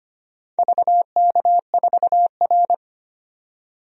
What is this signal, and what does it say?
Morse code, heard as a keyed tone, about 25 wpm, 700 Hz. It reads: VK4R
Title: DXCC-100 Call Signs - 4 Characters - 25wpm